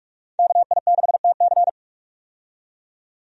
Morse code, heard as a keyed tone, about 46 wpm, 700 Hz. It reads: KI6TC
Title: USA Call Signs - 5 Characters - 45wpm